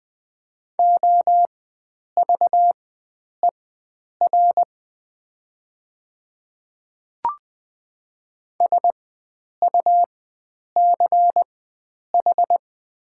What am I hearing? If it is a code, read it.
Morse code, heard as a keyed tone, about 20 wpm, spaced as for 9 wpm, 700 Hz. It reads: OVER SUCH
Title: 4x Character Spacing - Top 300 Words - 20wpm